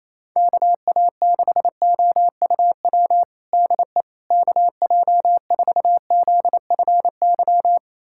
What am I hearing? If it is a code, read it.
Morse code, heard as a keyed tone, about 28 wpm, 700 Hz. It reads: KA6OUW DE KJ4ZFY
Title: USA Callsign DE Callsign - 6 Characters - 28wpm